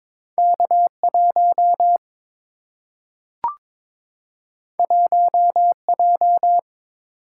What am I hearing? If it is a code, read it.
Morse code, heard as a keyed tone, about 22 wpm, 700 Hz. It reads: K1 1J
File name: practice / 22wpm / Two Letter-Number Combinations - 22wpm